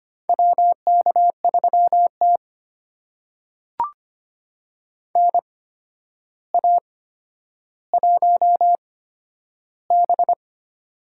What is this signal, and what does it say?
Morse code, heard as a keyed tone, about 25 wpm, spaced as for 11 wpm, 700 Hz. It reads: WK3T NA1B
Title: USA Call Signs - 4 Characters - 8x Character Spacing - 25wpm